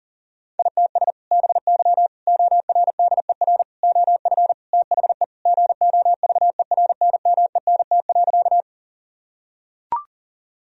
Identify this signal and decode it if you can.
Morse code, heard as a keyed tone, about 40 wpm, 700 Hz. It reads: ITS BY ORDER OF THE GOVERNMENT.
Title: Sentences from Top 1000 Words - Review Part 3 - 40wpm